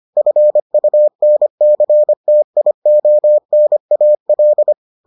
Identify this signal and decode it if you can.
Morse code, heard as a keyed tone, about 25 wpm, 600 Hz. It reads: FUNCTIONAL